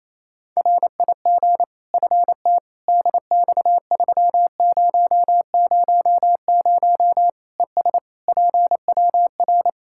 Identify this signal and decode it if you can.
Morse code, heard as a keyed tone, about 28 wpm, 700 Hz. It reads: RIG FTDX3000 ES PWR
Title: Practice QSO - Exchange 2 - Info - 28wpm